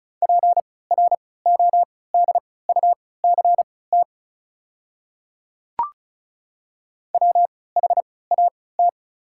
Morse code, heard as a keyed tone, about 36 wpm, spaced as for 20 wpm, 690 Hz. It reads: PRODUCT WHAT
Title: 3x Character Spacing - Top 500 Words - 35wpm